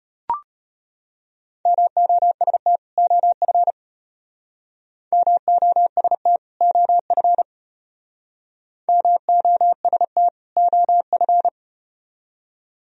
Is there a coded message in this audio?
Yes: Morse code from 1.7 to 11.5 s, heard as a keyed tone, about 33 wpm, 700 Hz. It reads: MOST OF MOST OF MOST OF